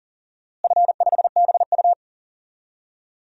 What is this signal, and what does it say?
Morse code, heard as a keyed tone, about 40 wpm, 700 Hz. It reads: F5BU
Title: DXCC-100 Call Signs - 4 Characters - Repeated 2x - 40wpm